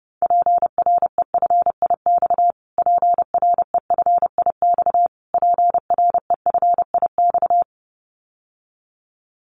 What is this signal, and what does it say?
Morse code, heard as a keyed tone, about 30 wpm, 700 Hz. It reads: PREFIX PREFIX PREFIX
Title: Callsign Prefixes for Top 25 DXCC - Repeated 3x - 30wpm